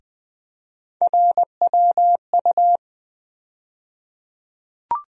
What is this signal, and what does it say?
Morse code, heard as a keyed tone, about 20 wpm, 700 Hz. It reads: RWU
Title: USA Call Sign Postfixes - 3 Letters 20wpm